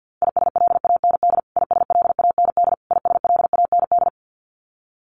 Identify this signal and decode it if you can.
Morse code, heard as a keyed tone, about 48 wpm, 700 Hz. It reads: ISLAND ISLAND ISLAND
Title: Top 500 Words Repeated 50wpm